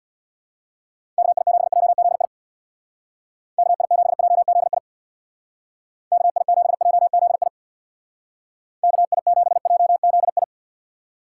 Morse code, heard as a keyed tone, about 60 wpm, 700 Hz. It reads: KI7JZI KI7JZI KI7JZI KI7JZI
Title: Speed Racing - USA 2x3 Call Signs 50wpm